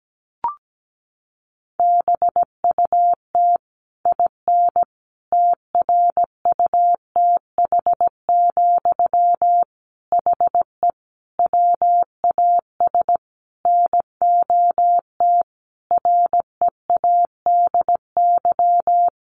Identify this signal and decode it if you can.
Morse code, heard as a keyed tone, about 17 wpm, 700 Hz. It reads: BUT IN TRUTH, HE WAS NOT READY
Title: Sentences from Top 1500 Words 17wpm